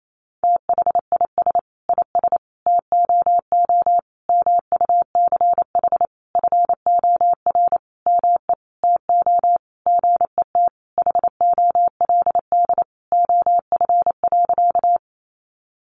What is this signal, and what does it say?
Morse code, heard as a keyed tone, about 28 wpm, 700 Hz. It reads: THIS IS TOO MUCH FOR ME TO GET HOLD OF.
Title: A Sound Of Thunder 28wpm